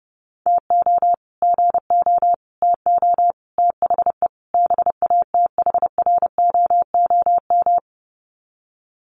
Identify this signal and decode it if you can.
Morse code, heard as a keyed tone, about 30 wpm, 700 Hz. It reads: TO GO TO THE BATHROOM
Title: Sets of 5 Words Encore 30wpm